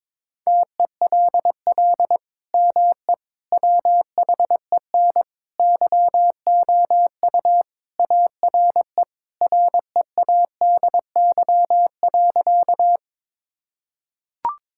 Morse code, heard as a keyed tone, about 22 wpm, 700 Hz. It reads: TELL ME WHEN YOU ARE READY.
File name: practice / 22wpm / Sentences with 2-Word Phrases - 22wpm